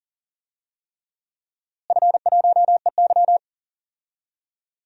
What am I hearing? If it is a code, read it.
Morse code, heard as a keyed tone, about 40 wpm, 700 Hz. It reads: F1EY
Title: DXCC-100 Call Signs - 4 Characters - Repeated 3x - 40wpm